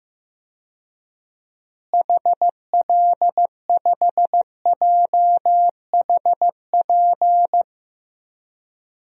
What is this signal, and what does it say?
Morse code, heard as a keyed tone, about 16 wpm, 700 Hz. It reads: HL5JHP
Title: DXCC-100 Call Signs - 6 Characters - Repeated 2x - 15wpm